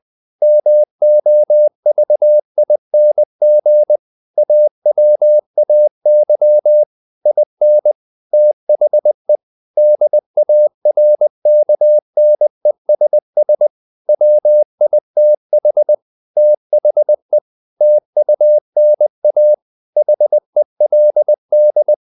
Morse code, heard as a keyed tone, about 20 wpm, 600 Hz. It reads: MOVING AWAY IN THE DARKNESS WITH THE TUNA HELD
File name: the old man and the sea / the old man and the sea part 1 20wpm